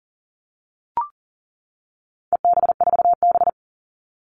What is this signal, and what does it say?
Morse code, heard as a keyed tone, about 40 wpm, 700 Hz. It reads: EB4B
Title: All Cut Numbers - 4-Digit 40wpm